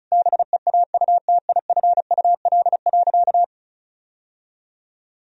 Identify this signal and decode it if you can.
Morse code, heard as a keyed tone, about 34 wpm, 700 Hz. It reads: BEAUTIFUL.
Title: Sentences from Top 1000 Words - Review Part 1 - 35wpm